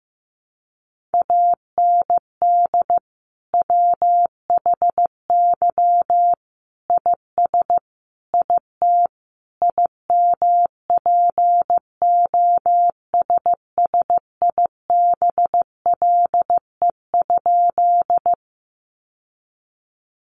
Morse code, heard as a keyed tone, about 15 wpm, 700 Hz. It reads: AND WHY IS IT IMPOSSIBLE?
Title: Sentences from Top 1100 Words 15wpm